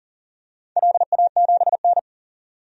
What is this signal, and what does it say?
Morse code, heard as a keyed tone, about 40 wpm, 700 Hz. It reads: LA7N
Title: DXCC-100 Call Signs - 4 Characters - 40wpm